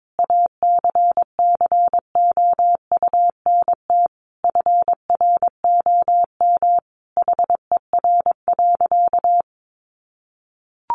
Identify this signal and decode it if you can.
Morse code, heard as a keyed tone, about 22 wpm, 700 Hz. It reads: ACCOUNT FROM HER.
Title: Sentences from Top 900 Words 22wpm